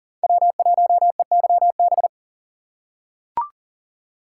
Morse code, heard as a keyed tone, about 40 wpm, 700 Hz. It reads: W1EYB
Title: USA Call Signs - 5 Characters - 2x Character Spacing - 40wpm